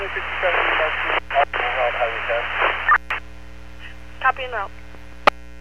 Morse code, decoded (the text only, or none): none